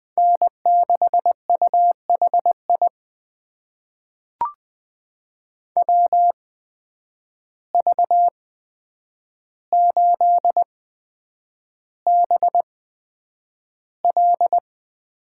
Morse code, heard as a keyed tone, about 20 wpm, spaced as for 9 wpm, 700 Hz. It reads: N6UHI WV8BL
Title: USA Call Signs - 5 Characters - 8x Character Spacing - 20wpm